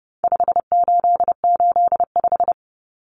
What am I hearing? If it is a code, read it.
Morse code, heard as a keyed tone, about 30 wpm, 700 Hz. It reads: 5885
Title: Numbers - 4-Digit 30wpm